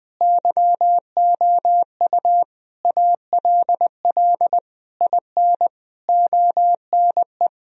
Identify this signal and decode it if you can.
Morse code, heard as a keyed tone, about 20 wpm, 700 Hz. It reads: YOU ALL IN ONE
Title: Sentences from Top 1000 Words - Review Part 1 - 20wpm